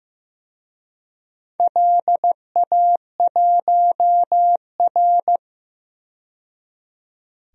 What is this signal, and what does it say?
Morse code, heard as a keyed tone, about 15 wpm, 700 Hz. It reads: LA1R
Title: DXCC-100 Call Signs - 4 Characters - 15wpm